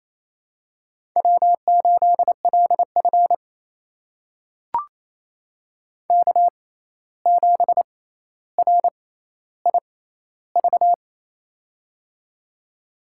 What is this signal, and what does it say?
Morse code, heard as a keyed tone, about 27 wpm, spaced as for 13 wpm, 700 Hz. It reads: W8LF K7RIV
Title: USA Call Signs - 6x Character Spacing - 28wpm